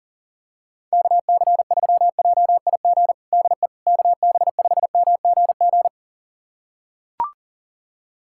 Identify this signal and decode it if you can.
Morse code, heard as a keyed tone, about 37 wpm, 700 Hz. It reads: KC3JIG DE KB5MGG
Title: USA Callsign DE Callsign 40wpm